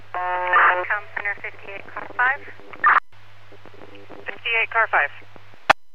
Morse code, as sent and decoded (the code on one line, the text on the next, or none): none